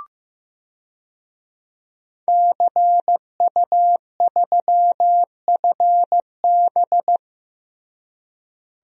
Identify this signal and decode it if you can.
Morse code, heard as a keyed tone, about 15 wpm, 700 Hz. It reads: CU3FB
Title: DXCC-100 Call Signs - 5 Characters - Repeated 2x - 15wpm